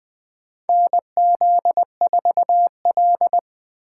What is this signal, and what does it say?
Morse code, heard as a keyed tone, about 20 wpm, 700 Hz. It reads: NZ4L
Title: USA Call Signs - 6x Character Spacing - 20wpm